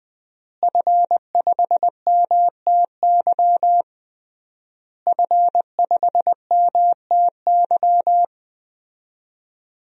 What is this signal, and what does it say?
Morse code, heard as a keyed tone, about 20 wpm, 700 Hz. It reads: F5MTY F5MTY
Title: DXCC-100 Call Signs - 5 Characters - Repeated 3x - 20wpm